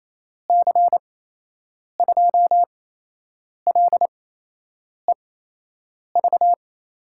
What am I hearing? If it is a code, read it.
Morse code, heard as a keyed tone, about 28 wpm, spaced as for 8 wpm, 700 Hz. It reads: C2LEV
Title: USA Call Signs - 6 Characters - 8x Character Spacing - 28wpm